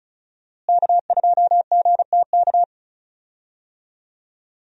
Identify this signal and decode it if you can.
Morse code, heard as a keyed tone, about 36 wpm, 700 Hz. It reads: K2GTK
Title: USA Call Signs - 5 Characters - Repeated 3x - 35wpm